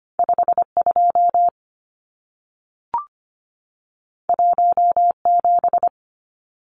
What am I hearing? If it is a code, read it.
Morse code, heard as a keyed tone, about 25 wpm, 700 Hz. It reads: 52 17
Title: Numbers - 2-Digit 25wpm